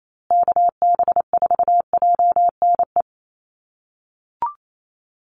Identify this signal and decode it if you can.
Morse code, heard as a keyed tone, about 28 wpm, 700 Hz. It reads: KB4JNE
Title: USA Call Signs Repeated 3x 28wpm